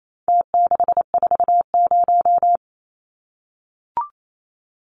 Morse code, heard as a keyed tone, about 28 wpm, 700 Hz. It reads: T640
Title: Numbers - Contest Style - 4-Digit 28wpm